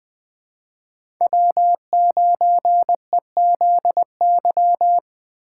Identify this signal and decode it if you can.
Morse code, heard as a keyed tone, about 20 wpm, 700 Hz. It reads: W9EZY
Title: USA Call Signs - 5 Characters - 3x Character Spacing - 20wpm